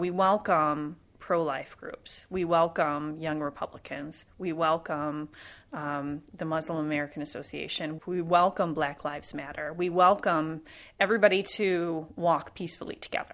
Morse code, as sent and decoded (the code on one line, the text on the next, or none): none